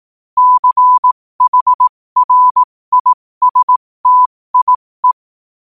Morse code, heard as a keyed tone, about 18 wpm, spaced as for 15 wpm, 1000 Hz. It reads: CHRISTIE
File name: qotdfiles / 15wpm